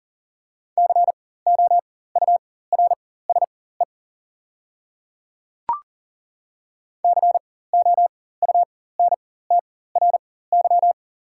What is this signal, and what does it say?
Morse code, heard as a keyed tone, about 40 wpm, spaced as for 19 wpm, 700 Hz. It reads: COURSE COUNTRY
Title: 4x Character Spacing - Top 500 Words - 40wpm